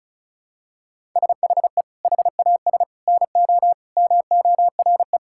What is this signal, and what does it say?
Morse code, heard as a keyed tone, about 36 wpm, 700 Hz. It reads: SHE HAS NO MORE
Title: Sentences from Top 1000 Words - Review Part 6 - 35wpm